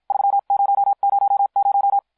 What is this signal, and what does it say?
Morse code, heard as a keyed tone, about 50 wpm, spaced as for 43 wpm, 800 Hz. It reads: 3000